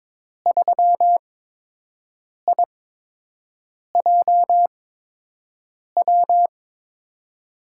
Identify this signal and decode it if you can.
Morse code, heard as a keyed tone, about 22 wpm, spaced as for 6 wpm, 700 Hz. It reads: 3IJW
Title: USA Call Signs - 5 Characters - 8x Character Spacing - 22wpm